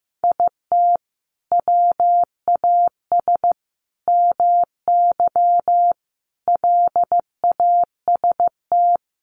Morse code, heard as a keyed tone, about 15 wpm, 700 Hz. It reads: IT WAS MY LAST
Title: Sentences from Top 1200 Words 15wpm